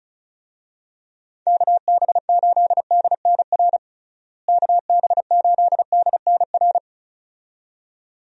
Morse code, heard as a keyed tone, about 36 wpm, 700 Hz. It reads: KB8DNR KB8DNR
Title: USA Call Signs - 6 Characters - Repeated 2x - 35wpm